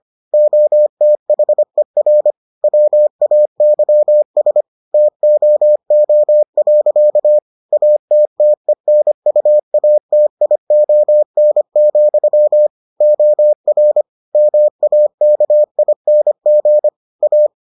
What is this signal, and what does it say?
Morse code, heard as a keyed tone, about 25 wpm, 600 Hz. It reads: OTHER WAYS TOO. ATTENUATION, OR MAKING A